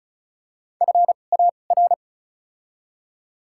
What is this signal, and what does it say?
Morse code, heard as a keyed tone, about 34 wpm, spaced as for 25 wpm, 700 Hz. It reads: FAR
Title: 2x Character Spacing - Top 500 Words - 35wpm